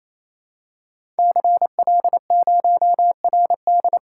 Morse code, heard as a keyed tone, about 27 wpm, 700 Hz. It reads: CL0RD
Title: Code Groups - 5 Letter:Numbers - 28wpm